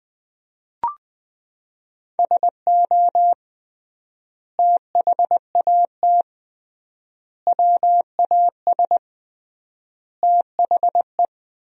Morse code, heard as a keyed tone, about 20 wpm, 700 Hz. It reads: SO THAT WAS THE
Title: 3x Word-Spacing - Sentences from Top 100 Words - 20wpm